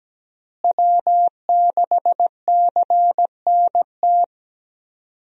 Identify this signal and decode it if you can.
Morse code, heard as a keyed tone, about 17 wpm, 700 Hz. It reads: W6CNT
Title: USA Call Signs - 5 Characters - Repeated 2x - 17wpm